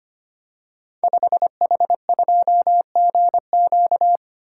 Morse code, heard as a keyed tone, about 25 wpm, 700 Hz. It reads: 5H2GQ
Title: Code Groups - 5 Letter:Numbers - 25wpm